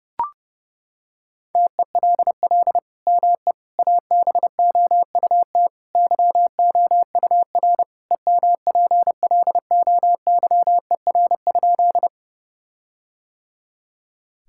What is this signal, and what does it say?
Morse code, heard as a keyed tone, about 32 wpm, 700 Hz. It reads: TELL ME ABOUT YOUR EMPLOYER?
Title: Sentences from Top 1600 Words 30wpm